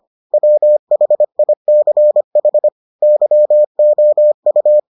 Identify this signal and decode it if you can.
Morse code, heard as a keyed tone, about 25 wpm, 600 Hz. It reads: WHICH YOU